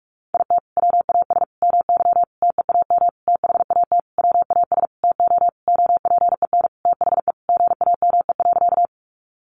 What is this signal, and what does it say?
Morse code, heard as a keyed tone, about 44 wpm, 700 Hz. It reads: IT WAS MY TEAM THAT WAS TO OPEN THE GAME.
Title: Sentences from Top 1500 Words 45wpm